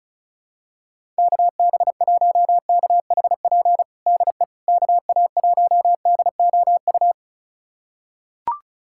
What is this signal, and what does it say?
Morse code, heard as a keyed tone, about 36 wpm, 700 Hz. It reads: KB1KHP DE KA1DOU
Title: USA Callsign DE Callsign 35wpm